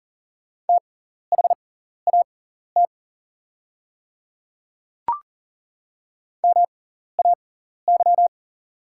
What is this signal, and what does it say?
Morse code, heard as a keyed tone, about 40 wpm, spaced as for 14 wpm, 690 Hz. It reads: THAT MAY
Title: 6x Character Spacing - Top 100 Words - 40wpm